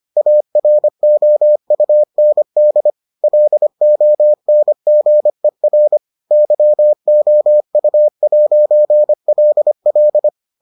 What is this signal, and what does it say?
Morse code, heard as a keyed tone, about 25 wpm, 600 Hz. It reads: AROUND LONGER YOU'LL